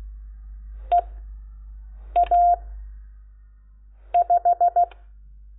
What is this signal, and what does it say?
Morse code, heard as a keyed tone, about 16 wpm, spaced as for 5 wpm, 700 Hz. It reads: EA5
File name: Lecciones / CLASE Nº10